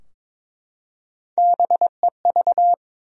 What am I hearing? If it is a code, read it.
Morse code, heard as a keyed tone, about 22 wpm, 700 Hz. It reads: BEV
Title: USA Common Names - Max 3 Letters - 22wpm